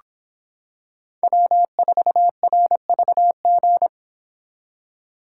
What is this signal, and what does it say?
Morse code, heard as a keyed tone, about 26 wpm, 700 Hz. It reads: W4RVG